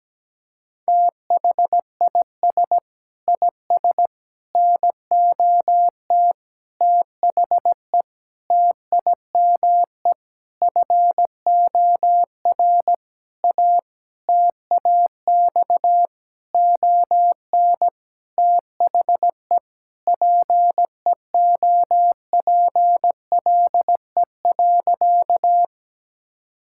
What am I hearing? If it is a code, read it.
Morse code, heard as a keyed tone, about 17 wpm, 700 Hz. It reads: THIS IS NOT THE TIME FOR A TAX ON THE PEOPLE.
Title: Sentences from Top 1000 Words - Review Part 5 - 17wpm